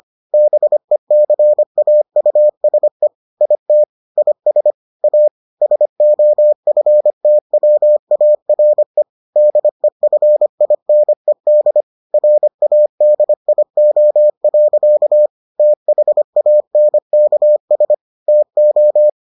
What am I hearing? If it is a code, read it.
Morse code, heard as a keyed tone, about 25 wpm, 600 Hz. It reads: BECAUSE IT IS A SOFTWARE DEFINED RADIO. THANKS TO